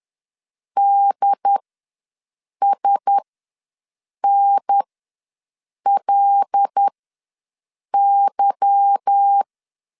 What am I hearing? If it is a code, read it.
Morse code, heard as a keyed tone, about 10 wpm, spaced as for 6 wpm, 800 Hz. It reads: DSNLY
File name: LEZIONI / 8 YJ45 / 8 Y ws 10wpm 3952